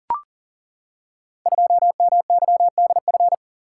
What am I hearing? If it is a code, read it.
Morse code, heard as a keyed tone, about 40 wpm, 700 Hz. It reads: 2MYDF
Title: Code Groups - 5 Letter:Numbers - 40wpm